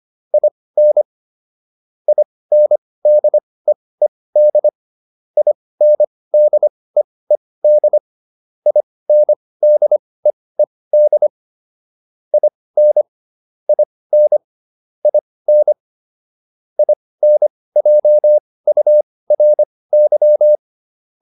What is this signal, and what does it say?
Morse code, heard as a keyed tone, about 25 wpm, spaced as for 18 wpm, 600 Hz. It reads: IN INDEED INDEED INDEED IN IN IN INJURY